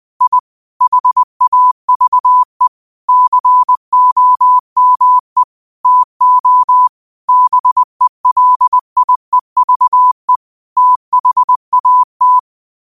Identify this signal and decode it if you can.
Morse code, heard as a keyed tone, about 20 wpm, 1000 Hz. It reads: I HAVE COME TO BELIEVE THAT
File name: qotdfiles / 20wpm